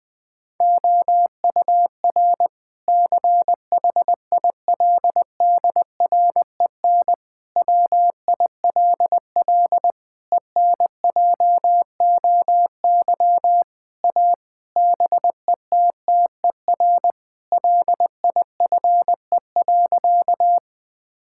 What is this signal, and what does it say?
Morse code, heard as a keyed tone, about 20 wpm, 700 Hz. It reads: OUR CHILDREN WILL ENJOY A BETTER LIFE.
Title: Sentences from Top 1000 Words - Review Part 5 - 20wpm